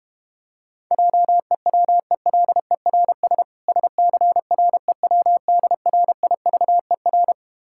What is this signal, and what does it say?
Morse code, heard as a keyed tone, about 32 wpm, 690 Hz. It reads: JEWELERS SCREWDRIVER